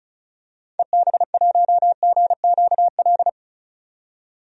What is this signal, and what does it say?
Morse code, heard as a keyed tone, about 34 wpm, 700 Hz. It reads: EB1GQL
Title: DXCC-100 Call Signs - 6 Characters - 35wpm